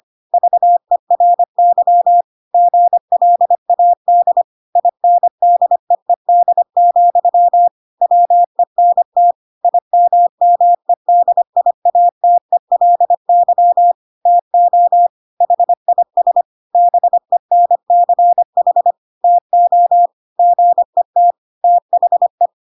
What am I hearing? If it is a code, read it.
Morse code, heard as a keyed tone, about 25 wpm, 700 Hz. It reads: VERY GLAD INDEED, WENT IMMEDIATELY TO HIS BENCH TO GET THE
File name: Pinocchio / Pinocchio ch2 25 25 ik1hge